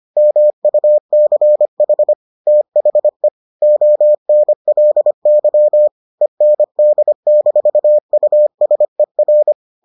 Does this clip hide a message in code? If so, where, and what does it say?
Yes: Morse code from 0.2 to 9.5 s, heard as a keyed tone, about 25 wpm, 600 Hz. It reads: MUCH THE ONLY END-USER